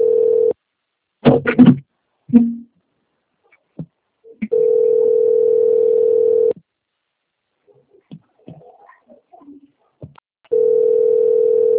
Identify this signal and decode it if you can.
Ringback tone (North America), 440 + 480 Hz.
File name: rg-1600-unknown- - - .25525